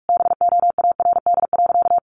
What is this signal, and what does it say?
Morse code, heard as a keyed tone, about 44 wpm, 700 Hz. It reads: BOARD.
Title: Sentences from Top 1100 Words 45wpm